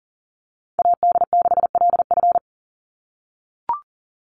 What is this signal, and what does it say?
Morse code, heard as a keyed tone, about 40 wpm, 700 Hz. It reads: AD6LF
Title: USA Call Signs 40wpm